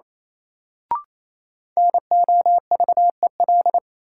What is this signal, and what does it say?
Morse code, heard as a keyed tone, about 28 wpm, 700 Hz. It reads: NOVEL
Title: Sentences from Top 1300 Words - New Word Drill Alt - 28wpm